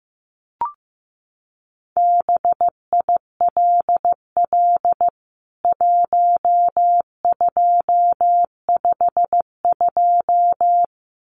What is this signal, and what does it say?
Morse code, heard as a keyed tone, about 15 wpm, 700 Hz. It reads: BILL 1252
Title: CWT Exchange 15wpm